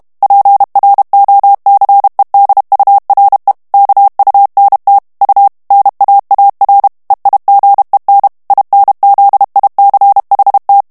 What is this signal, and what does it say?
Morse code, heard as a keyed tone, about 32 wpm, 800 Hz. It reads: PROCEDURE KUNT U NAAR EIGEN INZICHT
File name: HOLLANDAIS texte clair 2 vitesse 30 mots minute